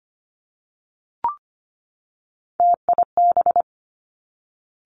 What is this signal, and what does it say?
Morse code, heard as a keyed tone, about 25 wpm, 700 Hz. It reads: TIB
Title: Three Challenging Letter-Number Combinations 25wpm